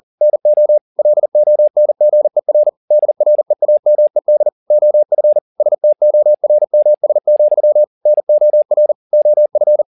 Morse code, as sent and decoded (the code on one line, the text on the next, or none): -. --- / .-.. --- -. --. .-. / -.. .-. .- -- . -.. / --- ..-. / ... - --- .-. -- ... --..-- / -. --- .-. / --- ..-.
NO LONGER DREAMED OF STORMS, NOR OF